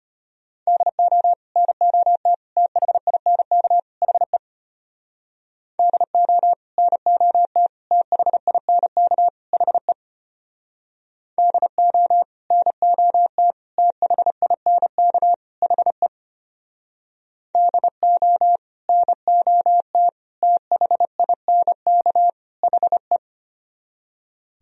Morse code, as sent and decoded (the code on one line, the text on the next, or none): -.. --- / -. --- - / - .... .. -. -.- / .... / -.. --- / -. --- - / - .... .. -. -.- / .... / -.. --- / -. --- - / - .... .. -. -.- / .... / -.. --- / -. --- - / - .... .. -. -.- / .... .
DO NOT THINK HE DO NOT THINK HE DO NOT THINK HE DO NOT THINK HE